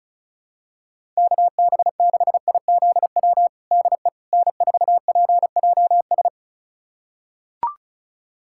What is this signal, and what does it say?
Morse code, heard as a keyed tone, about 34 wpm, 700 Hz. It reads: KB6IZW DE N4PJS
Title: USA Callsign DE Callsign 35wpm